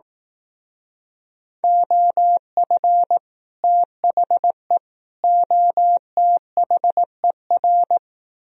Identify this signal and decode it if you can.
Morse code, heard as a keyed tone, about 18 wpm, 700 Hz. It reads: OF THE OTHER
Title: Speed-Racing - Sets of 3 Words 15wpm